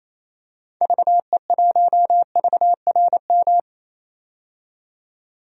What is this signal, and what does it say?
Morse code, heard as a keyed tone, about 28 wpm, 700 Hz. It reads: VE1VRM
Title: DXCC-100 Call Signs - 6 Characters - 28wpm